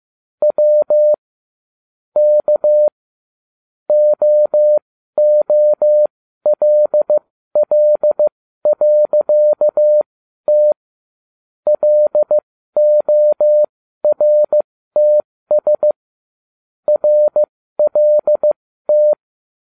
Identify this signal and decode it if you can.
Morse code, heard as a keyed tone, about 15 wpm, spaced as for 12 wpm, 600 Hz. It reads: W K OOLL.T LORTS RLT